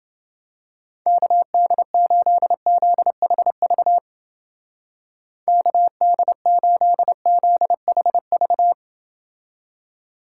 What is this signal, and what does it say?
Morse code, heard as a keyed tone, about 31 wpm, 700 Hz. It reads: KD8ZHV KD8ZHV